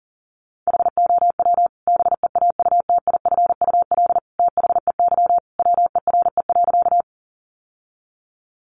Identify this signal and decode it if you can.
Morse code, heard as a keyed tone, about 37 wpm, 700 Hz. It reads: HOW BEAUTIFUL THEY WERE.
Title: Sentences from Top 900 Words 40wpm